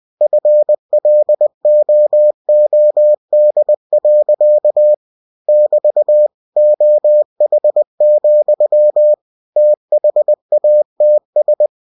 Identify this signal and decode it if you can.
Morse code, heard as a keyed tone, about 20 wpm, 600 Hz. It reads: FLOOD. <BT> OH, THATS
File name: cw-books / winnie-the-pooh / winnie-20wpm-page-0102